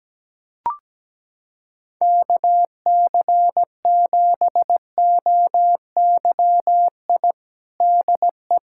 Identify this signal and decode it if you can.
Morse code, heard as a keyed tone, about 17 wpm, 700 Hz. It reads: KC7OYI DE
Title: USA Callsign DE Callsign - 6 Characters - 17wpm